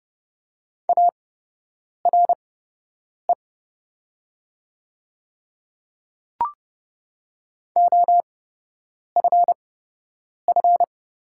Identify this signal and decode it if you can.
Morse code, heard as a keyed tone, about 31 wpm, spaced as for 8 wpm, 700 Hz. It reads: ARE OFF
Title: 8x Character Spacing - Top 300 Words - 30wpm